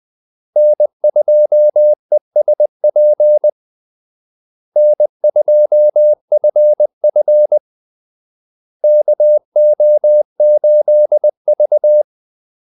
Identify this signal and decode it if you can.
Morse code, heard as a keyed tone, about 20 wpm, 600 Hz. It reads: N2ESP N2FF KO8V
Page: Kaldesignaler 20 wpm